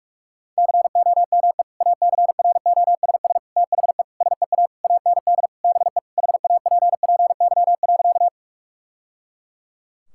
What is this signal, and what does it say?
Morse code, heard as a keyed tone, about 46 wpm, 700 Hz. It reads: COME ACROSS THE SEA AND BE HAPPY.
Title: Sentences from Top 1000 Words - Review Part 5 - 45wpm